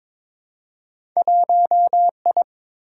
Morse code, heard as a keyed tone, about 22 wpm, 700 Hz. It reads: 1I
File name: practice / 22wpm / Two Letter-Number Combinations - 22wpm